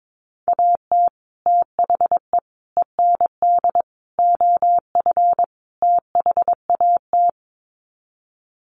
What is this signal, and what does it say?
Morse code, heard as a keyed tone, about 22 wpm, 700 Hz. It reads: AT THE END OF THAT
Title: Sets of 5 Words Encore2 22wpm